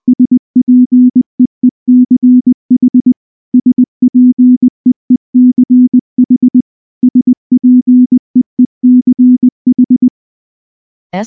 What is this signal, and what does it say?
Morse code, heard as a keyed tone, about 20 wpm, 250 Hz. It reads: SPEECH SPEECH SPEECH
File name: morse / morse-training-20WPM-5